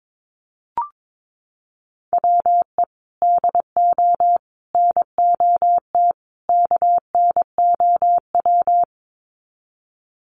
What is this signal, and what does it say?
Morse code, heard as a keyed tone, about 22 wpm, 700 Hz. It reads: WE DO NOT KNOW
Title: Sets of 4 Words 22wpm